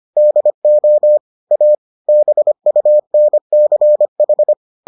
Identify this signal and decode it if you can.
Morse code, heard as a keyed tone, about 25 wpm, 600 Hz. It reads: DO A BUNCH